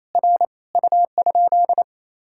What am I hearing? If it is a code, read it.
Morse code, heard as a keyed tone, about 27 wpm, 700 Hz. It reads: R U?
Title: Sentences from Top 100 Words and Abbreviations 28wpm